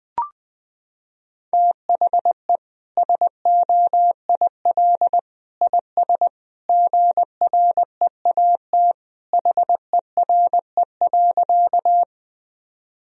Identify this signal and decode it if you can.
Morse code, heard as a keyed tone, about 21 wpm, 700 Hz. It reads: THE SOIL IS GREAT HERE.
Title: Sentences from Top 1000 Words - Review Part 4 - 20wpm